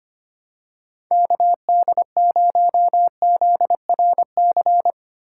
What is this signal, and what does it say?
Morse code, heard as a keyed tone, about 25 wpm, 700 Hz. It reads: KD0ZRC
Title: USA Call Signs - 3x Character Spacing - 25wpm